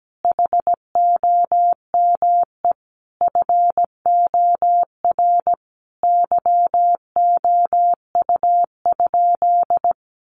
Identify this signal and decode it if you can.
Morse code, heard as a keyed tone, about 17 wpm, 700 Hz. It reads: HOME FOR YOU?
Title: Sentences from Top 1400 Words 17wpm